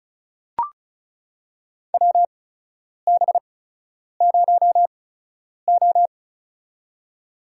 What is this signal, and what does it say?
Morse code, heard as a keyed tone, about 34 wpm, spaced as for 10 wpm, 700 Hz. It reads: WB0O
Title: USA Call Signs - 8x Character Spacing - 35wpm